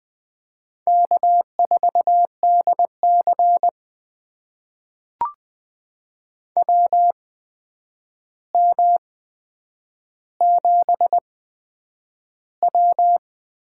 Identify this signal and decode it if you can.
Morse code, heard as a keyed tone, about 20 wpm, spaced as for 9 wpm, 700 Hz. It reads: K4DC WM7W
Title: USA Call Signs - 4 Characters - 8x Character Spacing - 20wpm